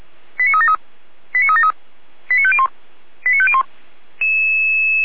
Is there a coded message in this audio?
Yes: five-tone selective call (ZVEI) from 0.4 to 3.6 s, heard as quick runs of five tones, tones of about 70 ms. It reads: 89373 89373 89501 89501